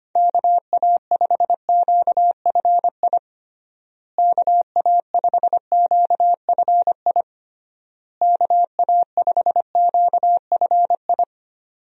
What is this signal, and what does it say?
Morse code, heard as a keyed tone, about 25 wpm, 700 Hz. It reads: KA5QFI KA5QFI KA5QFI
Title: USA Call Signs - 6 Characters - Repeated 3x - 25wpm